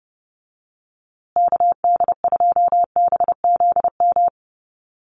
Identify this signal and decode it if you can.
Morse code, heard as a keyed tone, about 30 wpm, 700 Hz. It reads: KD2BZM
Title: USA Call Signs Repeated 2x 30wpm